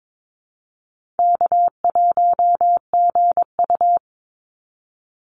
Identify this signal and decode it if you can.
Morse code, heard as a keyed tone, about 22 wpm, 700 Hz. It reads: K1GU
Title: Easy Call Signs Repeated 3x - 22wpm